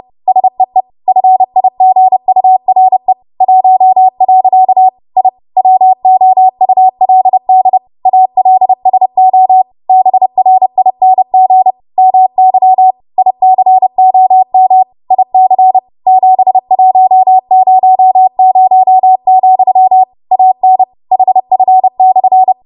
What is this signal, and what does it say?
Morse code, heard as a keyed tone, about 30 wpm, 750 Hz. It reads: SEE FIGURE 1. I WOULD ALSO BRING MY ICOM IC 7100, AN HF/
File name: Morse / 30WPM